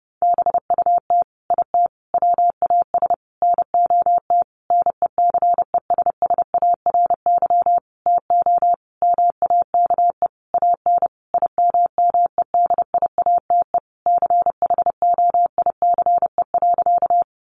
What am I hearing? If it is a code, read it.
Morse code, heard as a keyed tone, about 30 wpm, 700 Hz. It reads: BUT IT WAS NOT NECESSARY TO MAKE AN IMMEDIATE CHOICE.
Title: Sentences from Top 1200 Words 30wpm